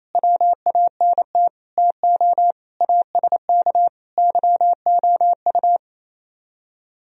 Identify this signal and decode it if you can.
Morse code, heard as a keyed tone, about 34 wpm, 700 Hz. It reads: WANT TO ASK YOU
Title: Speed-Racing - Sets of 5 Words 28wpm